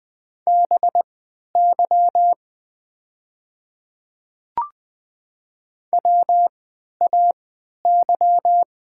Morse code, heard as a keyed tone, about 20 wpm, spaced as for 11 wpm, 700 Hz. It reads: BY WAY
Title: 3x Character Spacing - Top 200 Words - 20wpm